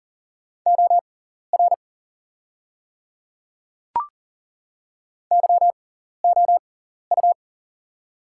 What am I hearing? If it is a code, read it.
Morse code, heard as a keyed tone, about 40 wpm, spaced as for 14 wpm, 690 Hz. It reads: OR YOU
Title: 6x Character Spacing - Top 200 Words - 40wpm